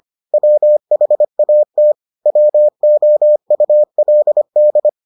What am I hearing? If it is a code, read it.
Morse code, heard as a keyed tone, about 25 wpm, 600 Hz. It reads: WHAT WOULD